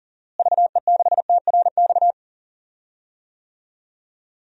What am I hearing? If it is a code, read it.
Morse code, heard as a keyed tone, about 40 wpm, 700 Hz. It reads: VE6TRX
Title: DXCC-100 Call Signs - 6 Characters - Repeated 2x - 40wpm